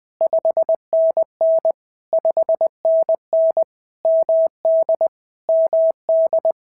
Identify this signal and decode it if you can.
Morse code, heard as a keyed tone, about 21 wpm, 650 Hz. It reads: 5NN 5NN MD MD
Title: POTA QSO Practice - 20wpm